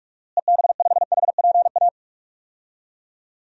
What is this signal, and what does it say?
Morse code, heard as a keyed tone, about 47 wpm, 700 Hz. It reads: EB5HPA
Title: DXCC-100 Call Signs - 6 Characters - Repeated 3x - 45wpm